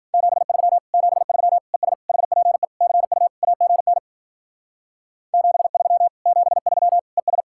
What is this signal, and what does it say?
Morse code, heard as a keyed tone, about 47 wpm, 700 Hz. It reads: 73 73 ES HPE CU AGN 73 73 ES
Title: Speed-Racing Common QSO Phrases - 40wpm